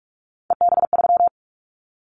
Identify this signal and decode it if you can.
Morse code, heard as a keyed tone, about 46 wpm, 700 Hz. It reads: EB3
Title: Callsign Prefixes for Top 25 DXCC - 45wpm